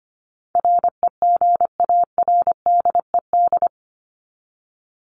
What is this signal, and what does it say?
Morse code, heard as a keyed tone, about 25 wpm, 700 Hz. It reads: REGARDED
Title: ED Suffix - Top 100 Words with Base Words 25wpm